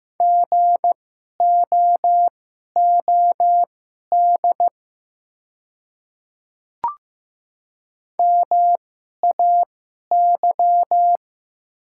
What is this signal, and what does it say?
Morse code, heard as a keyed tone, about 15 wpm, spaced as for 11 wpm, 700 Hz. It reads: GOOD MAY
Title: 2x Character Spacing - Top 200 Words - 15wpm